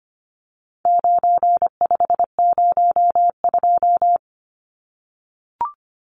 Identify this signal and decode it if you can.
Morse code, heard as a keyed tone, about 25 wpm, 700 Hz. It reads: 9502
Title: Numbers - 4-Digit 25wpm